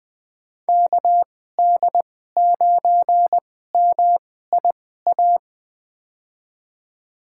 Morse code, heard as a keyed tone, about 20 wpm, spaced as for 14 wpm, 700 Hz. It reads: KD9MIA
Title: USA Call Signs - 6 Characters - 2x Character Spacing - 20wpm